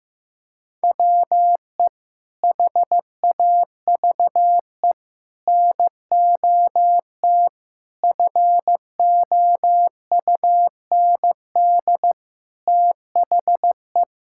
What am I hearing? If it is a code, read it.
Morse code, heard as a keyed tone, about 15 wpm, 700 Hz. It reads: WE HAVE NOT FOUND THE
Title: Sentences from Top 1000 Words - Review Part 6 - 15wpm